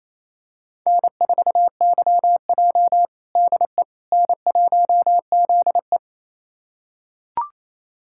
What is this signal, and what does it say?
Morse code, heard as a keyed tone, about 28 wpm, 700 Hz. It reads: N4YJ DE N1ZE
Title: USA Callsign DE Callsign - 4 Characters - 28wpm